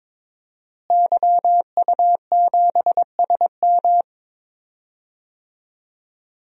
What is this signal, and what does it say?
Morse code, heard as a keyed tone, about 22 wpm, 700 Hz. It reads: YU7SM
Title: DXCC-100 Call Signs - 5 Characters - Repeated 3x - 22wpm